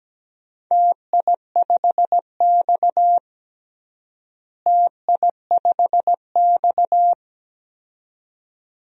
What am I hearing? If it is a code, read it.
Morse code, heard as a keyed tone, about 17 wpm, 700 Hz. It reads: TI5X TI5X
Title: DXCC-100 Call Signs - 4 Characters - Repeated 2x - 17wpm